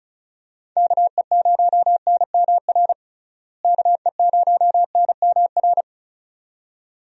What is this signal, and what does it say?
Morse code, heard as a keyed tone, about 36 wpm, 700 Hz. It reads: KE0NMR KE0NMR
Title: USA Call Signs - 6 Characters - Repeated 2x - 35wpm